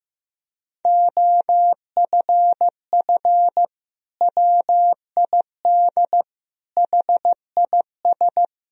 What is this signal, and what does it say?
Morse code, heard as a keyed tone, about 15 wpm, 700 Hz. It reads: OFF WID HIS
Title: Sentences from Top 300 Words and Abbreviations 15wpm